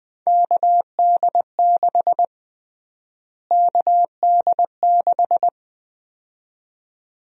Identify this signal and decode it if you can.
Morse code, heard as a keyed tone, about 20 wpm, 700 Hz. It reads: KD6 KD6
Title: USA Call Sign Prefixes - Repeated 2x - 20wpm